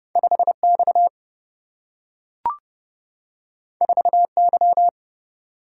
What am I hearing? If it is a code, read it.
Morse code, heard as a keyed tone, about 30 wpm, 700 Hz. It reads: 5X 4Y
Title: Single-Letter Single-Number 30wpm